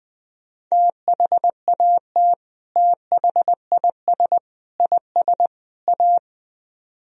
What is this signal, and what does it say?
Morse code, heard as a keyed tone, about 19 wpm, 700 Hz. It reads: THAT THIS IS A
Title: Speed-Racing - Sets of 4 Words 17wpm